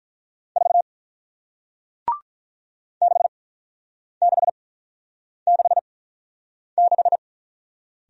Morse code, heard as a keyed tone, about 48 wpm, spaced as for 9 wpm, 700 Hz. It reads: 4 6666